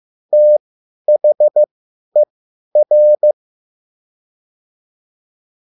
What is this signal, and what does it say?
Morse code, heard as a keyed tone, about 15 wpm, spaced as for 10 wpm, 600 Hz. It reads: THER